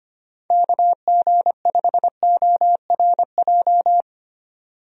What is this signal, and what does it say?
Morse code, heard as a keyed tone, about 25 wpm, 700 Hz. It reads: KG5ORJ
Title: USA Call Signs - 6 Characters - 25wpm